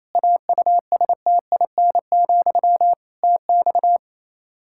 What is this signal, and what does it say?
Morse code, heard as a keyed tone, about 27 wpm, 700 Hz. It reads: AUSTIN, TX
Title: Practice QSO - Exchange 1 - Beginning - 28wpm